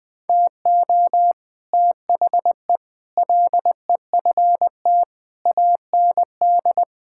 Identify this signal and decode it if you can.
Morse code, heard as a keyed tone, about 21 wpm, 700 Hz. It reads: TO THE LEFT AND
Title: Sets of 8 Words - 20wpm